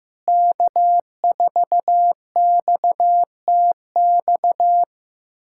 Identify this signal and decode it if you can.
Morse code, heard as a keyed tone, about 15 wpm, 700 Hz. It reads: K4XTX
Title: USA Callsign DE Callsign - 5 Characters - 15wpm